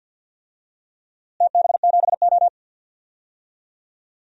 Morse code, heard as a keyed tone, about 53 wpm, 700 Hz. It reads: TB7O
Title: DXCC-100 Call Signs - 4 Characters - Repeated 2x - 50wpm